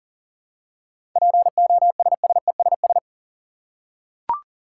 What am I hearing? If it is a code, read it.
Morse code, heard as a keyed tone, about 40 wpm, 700 Hz. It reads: POSSESS
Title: Seven Letter Words - 40wpm